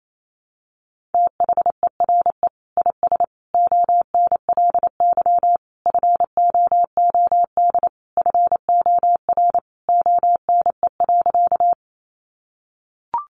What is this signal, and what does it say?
Morse code, heard as a keyed tone, about 28 wpm, 700 Hz. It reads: THERE IS ONLY FOOD FOR ONE.
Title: Sentences from Top 300 Words 28wpm